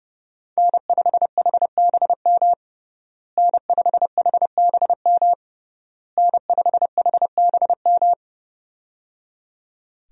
Morse code, heard as a keyed tone, about 30 wpm, 700 Hz. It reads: N5HBM N5HBM N5HBM
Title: USA Call Signs - 5 Characters - Repeated 3x - 30wpm